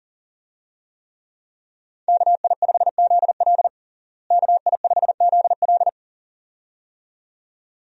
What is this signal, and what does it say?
Morse code, heard as a keyed tone, about 40 wpm, 700 Hz. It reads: KI5ZL KI5ZL
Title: USA Call Signs - 5 Characters - Repeated 2x - 40wpm